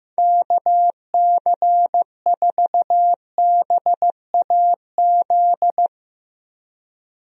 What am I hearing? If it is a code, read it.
Morse code, heard as a keyed tone, about 15 wpm, 700 Hz. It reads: KC4BAZ
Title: USA Call Signs - 6 Characters - Repeated 3x - 15wpm